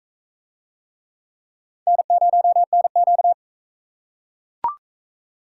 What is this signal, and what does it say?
Morse code, heard as a keyed tone, about 43 wpm, 700 Hz. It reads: N0NQ